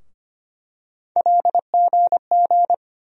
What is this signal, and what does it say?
Morse code, heard as a keyed tone, about 25 wpm, 700 Hz. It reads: LGG
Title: USA Call Sign Postfixes - 3 Letters 25wpm